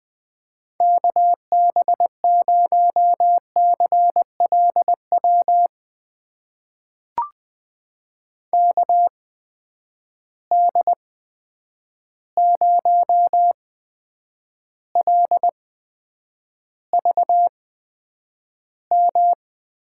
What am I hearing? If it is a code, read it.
Morse code, heard as a keyed tone, about 20 wpm, spaced as for 9 wpm, 700 Hz. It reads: KB0CLW KD0LVM
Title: USA Call Signs - 6 Characters - 8x Character Spacing - 20wpm